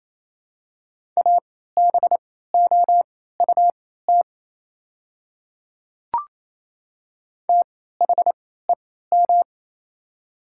Morse code, heard as a keyed tone, about 28 wpm, spaced as for 16 wpm, 700 Hz. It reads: ABOUT THEM
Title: 3x Character Spacing - Top 100 Words - 28wpm